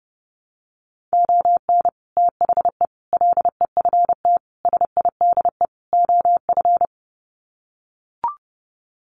Morse code, heard as a keyed tone, about 30 wpm, 700 Hz. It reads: ON THE LEFT SIDE OF
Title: Sets of 5 Words Encore2 30wpm